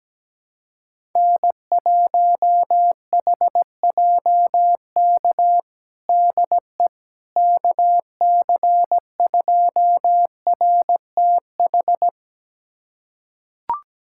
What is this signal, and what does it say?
Morse code, heard as a keyed tone, about 17 wpm, 700 Hz. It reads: N1HJK DE KC2RTH
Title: USA Callsign DE Callsign 17wpm